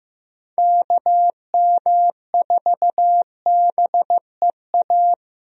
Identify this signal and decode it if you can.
Morse code, heard as a keyed tone, about 15 wpm, 700 Hz. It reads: KM4BEA
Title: USA Call Signs - 6 Characters - 15wpm